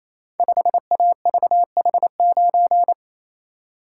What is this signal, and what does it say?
Morse code, heard as a keyed tone, about 28 wpm, 700 Hz. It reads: 5AVH9
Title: Code Groups - 5 Letter:Numbers - 28wpm